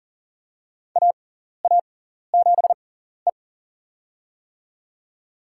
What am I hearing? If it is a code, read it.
Morse code, heard as a keyed tone, about 40 wpm, spaced as for 14 wpm, 700 Hz. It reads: AA7E
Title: USA Call Signs - 4 Characters - 6x Character Spacing - 40wpm